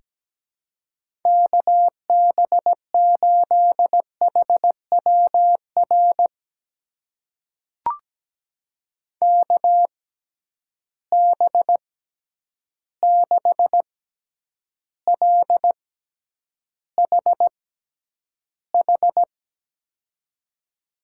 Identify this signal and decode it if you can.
Morse code, heard as a keyed tone, about 17 wpm, spaced as for 9 wpm, 700 Hz. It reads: KB8HWR KB6LHH
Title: USA Call Signs - 6 Characters - 6x Character Spacing - 17wpm